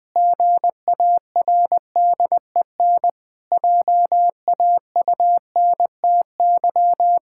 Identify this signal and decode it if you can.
Morse code, heard as a keyed tone, about 20 wpm, 700 Hz. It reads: GARDEN JAUNTY
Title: Pangram Challenge v2 - 20wpm